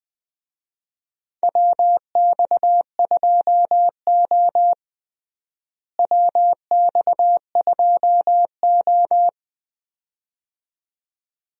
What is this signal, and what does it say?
Morse code, heard as a keyed tone, about 21 wpm, 700 Hz. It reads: WX2O WX2O
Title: USA Call Signs - 4 Characters - Repeated 2x - 20wpm